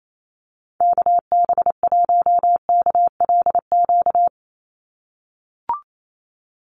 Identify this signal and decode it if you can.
Morse code, heard as a keyed tone, about 28 wpm, 700 Hz. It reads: KB1KLQ
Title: USA Call Signs 28wpm